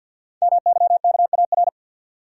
Morse code, heard as a keyed tone, about 50 wpm, 690 Hz. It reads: MYKAR
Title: Code Groups - 5 Letters - 50wpm